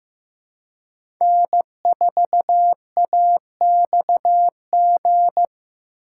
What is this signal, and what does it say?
Morse code, heard as a keyed tone, about 15 wpm, 700 Hz. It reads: N4AXG
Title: USA Call Signs - 5 Characters - 4x Character Spacing - 15wpm